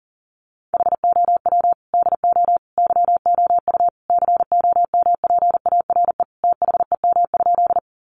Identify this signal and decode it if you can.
Morse code, heard as a keyed tone, about 40 wpm, 700 Hz. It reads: HOW DO YOU COMPARE THEM?
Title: Sentences from Top 1300 Words 40wpm